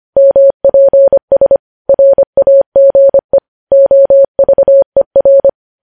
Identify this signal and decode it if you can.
Morse code, heard as a keyed tone, about 25 wpm, 550 Hz. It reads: MPS RAGE OVER
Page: news headlines in morse code at 25 wpm